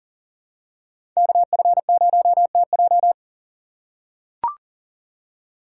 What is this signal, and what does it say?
Morse code, heard as a keyed tone, about 38 wpm, 700 Hz. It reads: KF0TJ
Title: USA Call Signs - 5 Characters - 6x Character Spacing - 40wpm